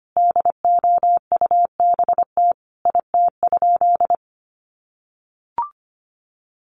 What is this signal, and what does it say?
Morse code, heard as a keyed tone, about 25 wpm, 700 Hz. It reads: DOUBT IT?
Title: Sentences from Top 1500 Words 25wpm